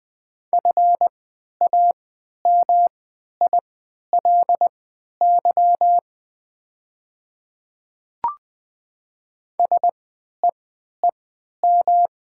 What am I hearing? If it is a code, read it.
Morse code, heard as a keyed tone, about 20 wpm, spaced as for 11 wpm, 700 Hz. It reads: FAMILY SEEM
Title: 3x Character Spacing - Top 500 Words - 20wpm